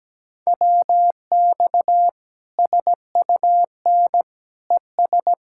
Morse code, heard as a keyed tone, about 17 wpm, 700 Hz. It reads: WX SUN ES
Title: Practice QSO - Exchange 2 - Info - 17wpm